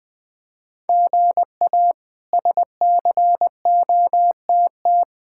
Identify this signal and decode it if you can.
Morse code, heard as a keyed tone, about 20 wpm, 700 Hz. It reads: GA SCOTT
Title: K1USN SST QSO Practice - 20wpm